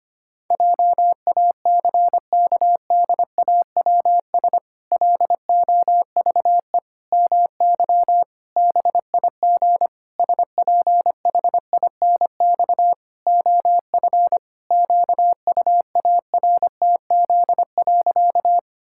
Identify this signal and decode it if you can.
Morse code, heard as a keyed tone, about 25 wpm, 700 Hz. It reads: JACKDAWS LOVE MY BIG SPHINX OF QUARTZ.
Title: Pangram Challenge v2 - 25wpm